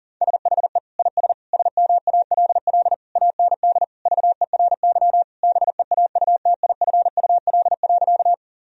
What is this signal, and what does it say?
Morse code, heard as a keyed tone, about 40 wpm, 700 Hz. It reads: SHE IS SMALL AND VERY BEAUTIFUL.
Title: Sentences from Top 1000 Words - Review Part 1 - 40wpm